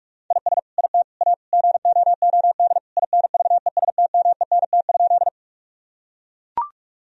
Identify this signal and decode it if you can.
Morse code, heard as a keyed tone, about 42 wpm, 700 Hz. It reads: IS IT A GOOD INVESTMENT?
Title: Sentences from Top 1000 Words - Review Part 4 - 45wpm